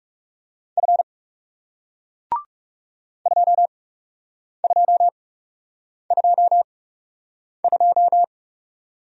Morse code, heard as a keyed tone, about 40 wpm, spaced as for 8 wpm, 700 Hz. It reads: F 2222